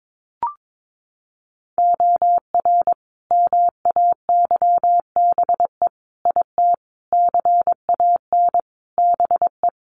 Morse code, heard as a keyed tone, about 22 wpm, 700 Hz. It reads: OR MAYBE IT CAN BE
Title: A Sound Of Thunder 22wpm